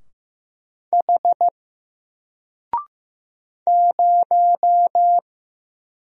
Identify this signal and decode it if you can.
Morse code, heard as a keyed tone, about 15 wpm, 700 Hz. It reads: H 0
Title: Single Letter-Number Alt 15wpm